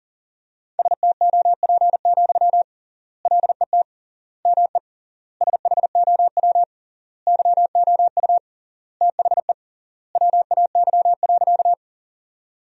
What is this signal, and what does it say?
Morse code, heard as a keyed tone, about 40 wpm, 700 Hz. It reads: STOP, LET ME SHOW YOU THE WAY.
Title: 3x Word-Spacing - Sentences from Top 300 Words - 40wpm